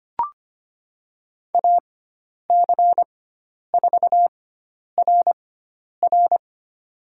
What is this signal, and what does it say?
Morse code, heard as a keyed tone, about 25 wpm, spaced as for 10 wpm, 700 Hz. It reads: AC4RR
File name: Lesson 16g - Letter c - Call Signs 25wpm